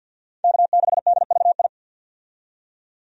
Morse code, heard as a keyed tone, about 50 wpm, 700 Hz. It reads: K6DVI
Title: USA Call Signs - 5 Characters - 50wpm